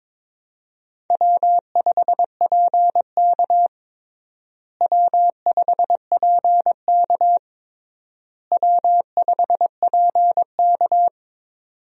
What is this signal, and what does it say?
Morse code, heard as a keyed tone, about 22 wpm, 700 Hz. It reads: W5PK W5PK W5PK
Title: USA Call Signs - 4 Characters - Repeated 3x - 22wpm